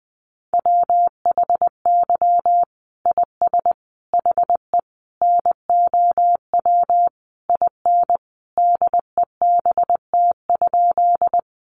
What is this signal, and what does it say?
Morse code, heard as a keyed tone, about 20 wpm, 700 Hz. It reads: WHY IS HE NOW IN DEBT?
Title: Sentences from Top 1300 Words 20wpm